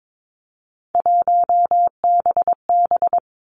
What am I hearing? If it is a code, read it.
Morse code, heard as a keyed tone, about 22 wpm, 700 Hz. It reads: 1BB
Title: Three Challenging Letter-Number Combinations 22wpm